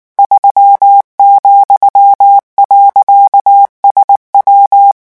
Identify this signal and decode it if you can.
Morse code, heard as a keyed tone, about 19 wpm, 800 Hz. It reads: 3,.SW
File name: morse mp3 / revision15-vitesse 18 mots